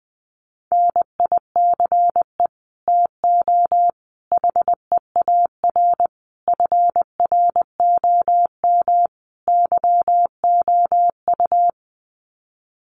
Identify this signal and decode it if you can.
Morse code, heard as a keyed tone, about 19 wpm, 700 Hz. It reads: NICE TO HEAR FROM YOU
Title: Common QSO Phrases 20wpm